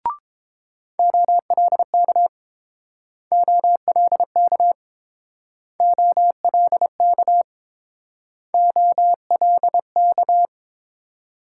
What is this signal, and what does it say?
Morse code, heard as a keyed tone, about 29 wpm, 700 Hz. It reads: OLK OLK OLK OLK